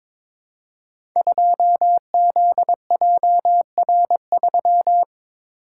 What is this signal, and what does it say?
Morse code, heard as a keyed tone, about 22 wpm, 700 Hz. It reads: 2ZJR3
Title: Code Groups - 5 Letter:Numbers - 22wpm